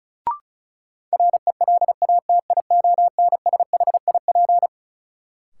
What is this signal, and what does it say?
Morse code, heard as a keyed tone, about 36 wpm, 700 Hz. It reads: RELATIONSHIP
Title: Sentences from Top 1100 Words - New Word Drill - 35wpm